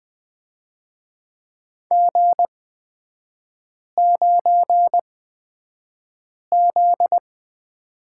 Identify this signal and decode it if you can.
Morse code, heard as a keyed tone, about 20 wpm, spaced as for 5 wpm, 700 Hz. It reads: G9Z
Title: 243 Lesson 39e - Letter z - Three Characters 20wpm